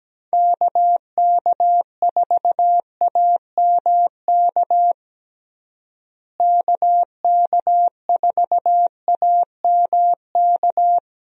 Morse code, heard as a keyed tone, about 17 wpm, 700 Hz. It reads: KK4AMK KK4AMK
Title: USA Call Signs - 6 Characters - Repeated 3x - 17wpm